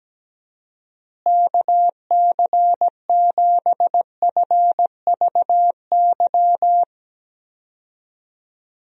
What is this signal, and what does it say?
Morse code, heard as a keyed tone, about 17 wpm, 700 Hz. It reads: KC7FVY
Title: USA Call Signs - 6 Characters - Repeated 2x - 17wpm